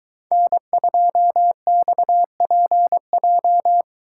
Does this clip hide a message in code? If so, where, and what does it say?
Yes: Morse code from 0.3 to 3.8 s, heard as a keyed tone, about 22 wpm, 700 Hz. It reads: N2XPJ